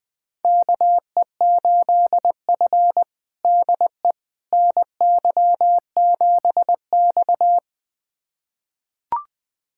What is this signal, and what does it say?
Morse code, heard as a keyed tone, about 20 wpm, 700 Hz. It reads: KE8F DE NY7X
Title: USA Callsign DE Callsign - 4 Characters - 20wpm